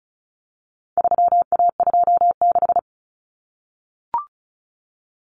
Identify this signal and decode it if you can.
Morse code, heard as a keyed tone, about 34 wpm, 700 Hz. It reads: 3A26
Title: Common Cut Numbers - 4-Digit 35wpm